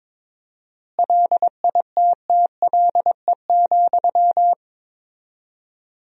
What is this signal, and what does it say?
Morse code, heard as a keyed tone, about 22 wpm, 700 Hz. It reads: LITTLE,
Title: 4x Word-Spacing - Sentences from Top 400 Words - 22wpm